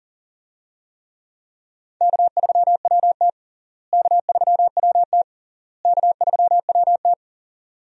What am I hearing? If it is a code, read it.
Morse code, heard as a keyed tone, about 40 wpm, 700 Hz. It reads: K3WT K3WT K3WT
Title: USA Call Signs - 4 Characters - Repeated 3x - 40wpm